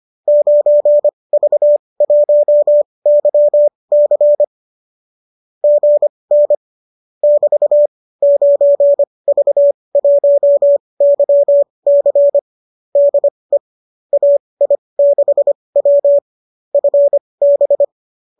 Morse code, heard as a keyed tone, about 25 wpm, spaced as for 20 wpm, 590 Hz. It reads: 9V1YC GN <BT> 9V1YC DE AI6W FB